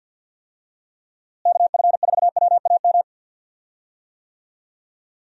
Morse code, heard as a keyed tone, about 53 wpm, 700 Hz. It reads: KF4WAM
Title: USA Call Signs - 6 Characters - Repeated 2x - 50wpm